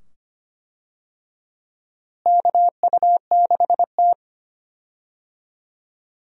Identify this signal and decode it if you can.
Morse code, heard as a keyed tone, about 25 wpm, 700 Hz. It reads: KU6T
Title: USA Call Signs - 4 Characters - Repeated 2x - 25wpm